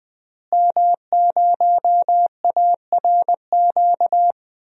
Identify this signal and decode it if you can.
Morse code, heard as a keyed tone, about 20 wpm, 700 Hz. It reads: M0ARQ
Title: DXCC-100 Call Signs - 5 Characters - Repeated 2x - 20wpm